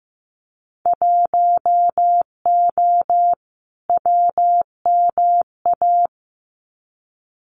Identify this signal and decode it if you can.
Morse code, heard as a keyed tone, about 15 wpm, 700 Hz. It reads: 1O WMA
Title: Winter Field Day Exchanges 15wpm